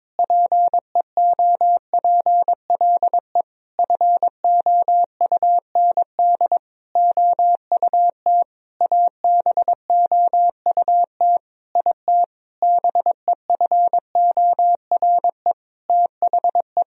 Morse code, heard as a keyed tone, about 22 wpm, 700 Hz. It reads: PEOPLE FOUND OUT ABOUT IT BEFORE THE
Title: Sentences from Top 1000 Words - Review Part 3 - 22wpm